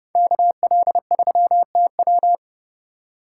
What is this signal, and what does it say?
Morse code, heard as a keyed tone, about 30 wpm, 700 Hz. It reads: KL3TW